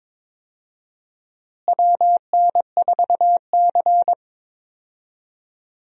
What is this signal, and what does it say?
Morse code, heard as a keyed tone, about 22 wpm, 700 Hz. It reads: WN4C
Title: USA Call Signs - 4 Characters - Repeated 3x - 22wpm